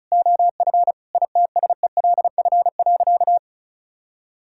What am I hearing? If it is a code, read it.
Morse code, heard as a keyed tone, about 36 wpm, 700 Hz. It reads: OF ITSELF.
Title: Sentences from Top 1000 Words - Review Part 5 - 35wpm